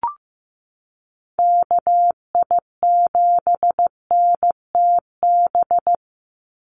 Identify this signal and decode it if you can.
Morse code, heard as a keyed tone, about 15 wpm, 700 Hz. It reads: KI7NTB
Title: USA Call Signs 15wpm